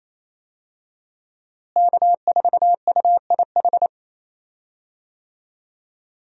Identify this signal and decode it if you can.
Morse code, heard as a keyed tone, about 28 wpm, 700 Hz. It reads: K4UIH
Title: USA Call Signs - 5 Characters - Repeated 2x - 28wpm